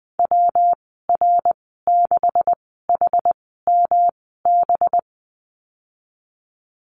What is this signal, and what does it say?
Morse code, heard as a keyed tone, about 20 wpm, spaced as for 14 wpm, 700 Hz. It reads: WR6HMB
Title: 213 Lesson 34g - Number 8 - Call Signs 20wpm